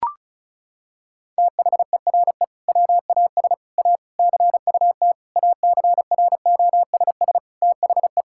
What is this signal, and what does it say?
Morse code, heard as a keyed tone, about 36 wpm, 700 Hz. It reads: THERE WAS A CUT ACROSS THE
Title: Sentences from Top 1000 Words - Review Part 5 - 35wpm